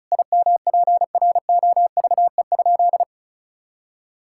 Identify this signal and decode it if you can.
Morse code, heard as a keyed tone, about 36 wpm, 690 Hz. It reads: IMPROVE?
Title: Sentences from Top 1000 Words - Review Part 5 - 35wpm